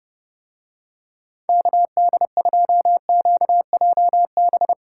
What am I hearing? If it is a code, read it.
Morse code, heard as a keyed tone, about 29 wpm, 700 Hz. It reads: KD2QJB